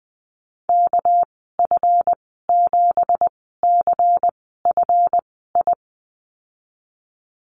Morse code, heard as a keyed tone, about 19 wpm, spaced as for 14 wpm, 700 Hz. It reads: KF7CFI
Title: 231 Lesson 37g - Letter x - Call Signs 20wpm